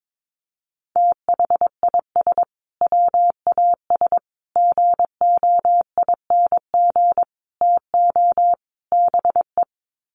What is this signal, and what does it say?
Morse code, heard as a keyed tone, about 22 wpm, 700 Hz. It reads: THIS WAS GOING TO BE
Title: Sets of 5 Words Encore2 22wpm